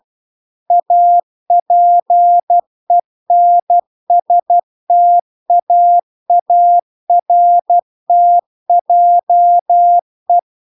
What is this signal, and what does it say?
Morse code, heard as a keyed tone, about 12 wpm, 700 Hz. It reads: APENSTAARTJE
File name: cw-cursus-VERON-a03